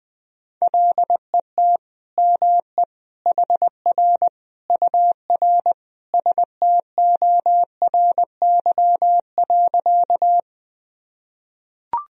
Morse code, heard as a keyed tone, about 21 wpm, 700 Hz. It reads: LET ME HR UR STORY.
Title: Sentences from Top 300 Words and Abbreviations 20wpm